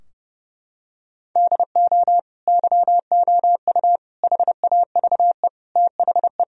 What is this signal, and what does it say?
Morse code, heard as a keyed tone, about 30 wpm, 700 Hz. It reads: DO YOU HAVE THE
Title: Sentences from Top 1000 Words - Review Part 3 - 30wpm